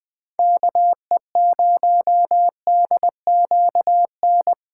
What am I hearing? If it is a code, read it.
Morse code, heard as a keyed tone, about 20 wpm, 700 Hz. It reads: KE0DQN
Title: USA Callsign DE Callsign - 6 Characters - 20wpm